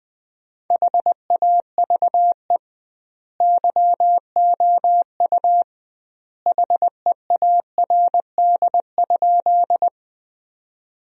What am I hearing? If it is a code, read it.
Morse code, heard as a keyed tone, about 21 wpm, 700 Hz. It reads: HAVE YOU HEARD?
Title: 2x Word-Spacing - Sentences from Top 400 Words - 20wpm